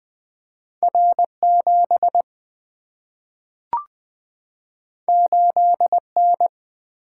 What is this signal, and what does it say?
Morse code, heard as a keyed tone, about 20 wpm, 700 Hz. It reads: R7 8N
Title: Single-Letter Single-Number 20wpm